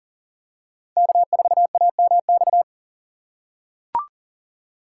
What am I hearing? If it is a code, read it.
Morse code, heard as a keyed tone, about 39 wpm, 700 Hz. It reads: K4AMX
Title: USA Call Signs - 5 Characters - 2x Character Spacing - 40wpm